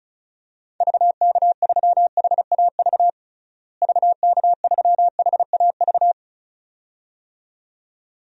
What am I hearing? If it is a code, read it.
Morse code, heard as a keyed tone, about 34 wpm, 700 Hz. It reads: VK3HAV VK3HAV
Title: DXCC-100 Call Signs - 6 Characters - Repeated 2x - 35wpm